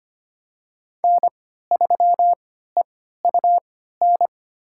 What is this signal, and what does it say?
Morse code, heard as a keyed tone, about 25 wpm, spaced as for 14 wpm, 700 Hz. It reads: N3EUN
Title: USA Call Signs - 5 Characters - 3x Character Spacing - 25wpm